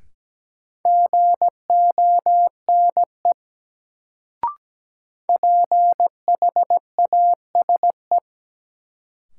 Morse code, heard as a keyed tone, about 17 wpm, 700 Hz. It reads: GONE PHASE
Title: Sentences from Top 1200 Words - New Word Drill - 17wpm